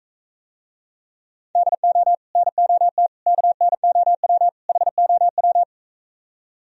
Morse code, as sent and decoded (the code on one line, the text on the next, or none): -.. --- / -. --- - / -.- -. --- .-- / .... --- .--
DO NOT KNOW HOW